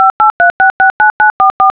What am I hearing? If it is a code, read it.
DTMF, heard as telephone touch tones, tones of about 130 ms. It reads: 583669944